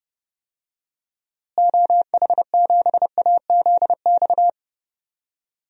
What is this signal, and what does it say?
Morse code, heard as a keyed tone, about 30 wpm, 700 Hz. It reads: OH7AZX
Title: DXCC-100 Call Signs - 6 Characters - Repeated 2x - 30wpm